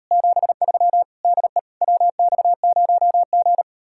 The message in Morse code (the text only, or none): 73 DE WX0G